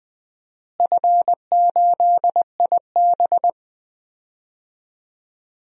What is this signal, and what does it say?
Morse code, heard as a keyed tone, about 20 wpm, 700 Hz. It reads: F8IB
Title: DXCC-100 Call Signs - 4 Characters - Repeated 3x - 20wpm